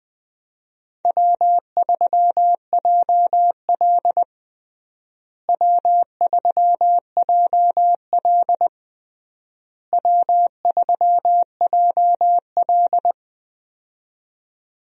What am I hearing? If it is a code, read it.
Morse code, heard as a keyed tone, about 20 wpm, 700 Hz. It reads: W3JL W3JL W3JL
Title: USA Call Signs - 4 Characters - Repeated 3x - 20wpm